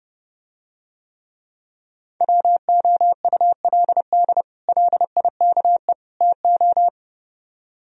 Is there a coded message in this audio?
Yes: Morse code from 2.2 to 6.9 s, heard as a keyed tone, about 29 wpm, 700 Hz. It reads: WOULD LIKE TO